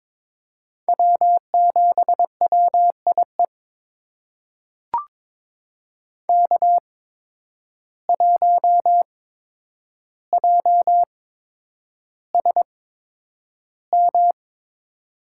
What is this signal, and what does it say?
Morse code, heard as a keyed tone, about 22 wpm, spaced as for 9 wpm, 700 Hz. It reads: W7WIE K1JSM
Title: USA Call Signs - 5 Characters - 8x Character Spacing - 22wpm